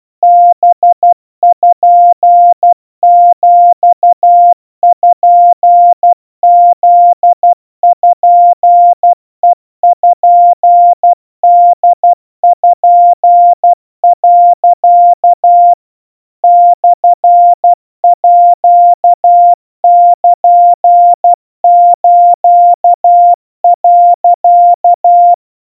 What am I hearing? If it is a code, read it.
Morse code (Japanese Wabun code), heard as a keyed tone, about 12 wpm, 700 Hz. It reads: パピプペポ、 モールス、
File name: kawa / Morse sample / iroha 40 cpm 700 Hz Crispy